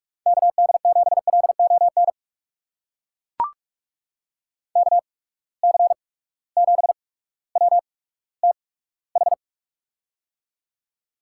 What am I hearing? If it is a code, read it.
Morse code, heard as a keyed tone, about 42 wpm, spaced as for 19 wpm, 700 Hz. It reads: KD7LON KC7WTH
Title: USA Call Signs - 6 Characters - 8x Character Spacing - 45wpm